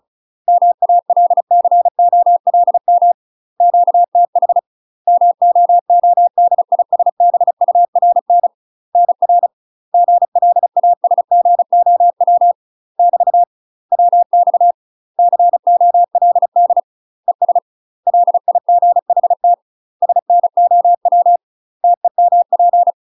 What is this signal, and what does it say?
Morse code, heard as a keyed tone, about 36 wpm, 700 Hz. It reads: MALCOLM QTH MOODISBURN NR GLASGOW <BT> WX COLD ES LIGHT SNOW TEMP
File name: UK MASTER QSOs TRAINING / 35WPM / UK MASTER QSOs 0033